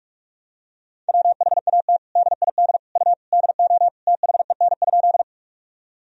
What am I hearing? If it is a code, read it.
Morse code, heard as a keyed tone, about 46 wpm, 700 Hz. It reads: WHAT DID U DO THEN?
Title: Sentences from Top 100 Words and Abbreviations 45wpm